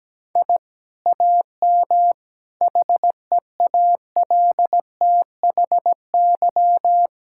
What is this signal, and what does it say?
Morse code, heard as a keyed tone, about 17 wpm, 700 Hz. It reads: I AM HEALTHY
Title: Sentences from Top 1000 Words - Review Part 5 - 17wpm